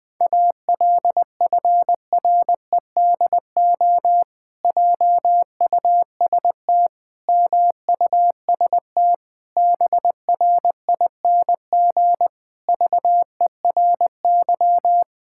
Morse code, heard as a keyed tone, about 20 wpm, 700 Hz. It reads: ALFREDO JUST MUST BRING VERY
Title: Pangram Challenge v2 - 20wpm